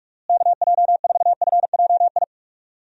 Morse code, heard as a keyed tone, about 47 wpm, 700 Hz. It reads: KJ4FJI
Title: USA Call Signs - 6 Characters - Repeated 2x - 45wpm